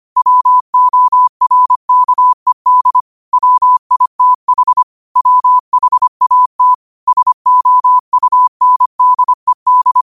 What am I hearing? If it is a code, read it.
Morse code, heard as a keyed tone, about 25 wpm, 1000 Hz. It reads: WORKED WITH WHAT SOUNDED